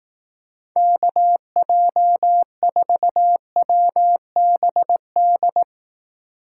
Morse code, heard as a keyed tone, about 18 wpm, 700 Hz. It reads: KJ4WBD
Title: Speed Racing - USA 2x3 Call Signs 15wpm